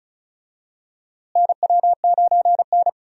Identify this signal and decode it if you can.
Morse code, heard as a keyed tone, about 36 wpm, 700 Hz. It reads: NW9N
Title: USA Call Signs - 4 Characters - Repeated 3x - 35wpm